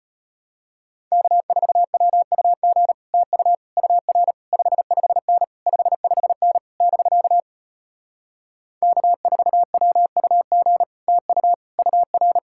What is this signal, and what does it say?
Morse code, heard as a keyed tone, about 33 wpm, 700 Hz. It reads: K4WUG TU UR 55N 55N <BK> K4WUG TU UR